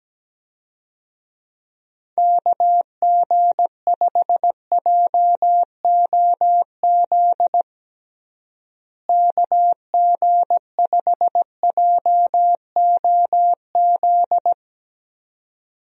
Morse code, heard as a keyed tone, about 17 wpm, 700 Hz. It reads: KG5JOZ KG5JOZ
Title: USA Call Signs - 6 Characters - Repeated 2x - 17wpm